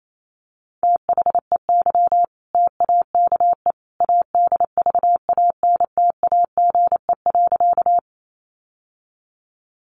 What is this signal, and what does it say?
Morse code, heard as a keyed tone, about 28 wpm, 700 Hz. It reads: THEY TAKE ADVANTAGE.
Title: Sentences from Top 900 Words 28wpm